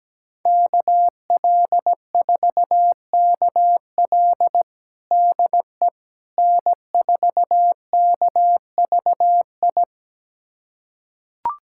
Morse code, heard as a keyed tone, about 17 wpm, 700 Hz. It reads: KL4KL DE N4KVI
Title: USA Callsign DE Callsign - 5 Characters - 17wpm